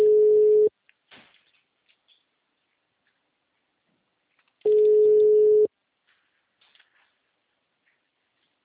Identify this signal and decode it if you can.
Ringback tone (Europe, 425 Hz).